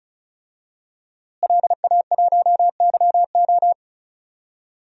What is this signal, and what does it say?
Morse code, heard as a keyed tone, about 34 wpm, 700 Hz. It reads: LA1YO
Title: DXCC-100 Call Signs - 5 Characters - Repeated 3x - 35wpm